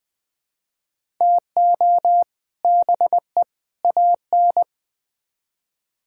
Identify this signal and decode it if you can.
Morse code, heard as a keyed tone, about 20 wpm, 700 Hz. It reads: TO BE AN
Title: Build A Phrase - 3-Word Phrases - 20wpm